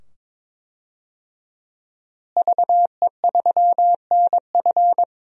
Morse code, heard as a keyed tone, about 22 wpm, 700 Hz. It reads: VE3NF
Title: DXCC-100 Call Signs - 5 Characters - Repeated 2x - 22wpm